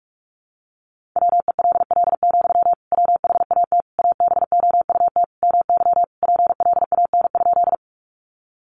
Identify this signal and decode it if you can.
Morse code, heard as a keyed tone, about 44 wpm, 700 Hz. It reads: WELL, WHAT ABOUT MY PLAN?
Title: Sentences from Top 500 Words 45wpm